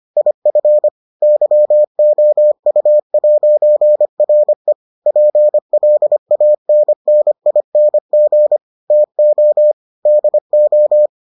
Morse code, heard as a keyed tone, about 25 wpm, 600 Hz. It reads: IF YOU'RE PLANNING TO DO